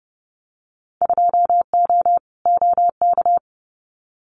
Morse code, heard as a keyed tone, about 30 wpm, 700 Hz. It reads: 2O OK
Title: Winter Field Day Exchanges 30wpm